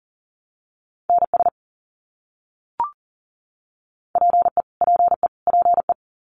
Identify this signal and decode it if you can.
Morse code, heard as a keyed tone, about 40 wpm, 700 Hz. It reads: NS PE PE PE
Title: Canadian Provinces and Territories - Repeated 3x - 40wpm